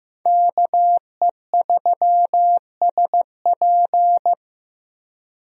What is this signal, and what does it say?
Morse code, heard as a keyed tone, about 15 wpm, 700 Hz. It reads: KE3SP
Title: USA Call Signs - 5 Characters - Repeated 3x - 15wpm